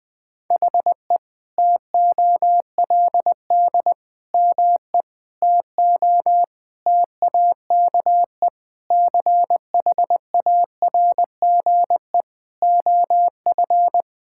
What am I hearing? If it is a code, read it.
Morse code, heard as a keyed tone, about 21 wpm, 700 Hz. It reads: HE TOLD ME TO TAKE CHARGE OF
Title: Sentences from Top 1000 Words - Review Part 6 - 20wpm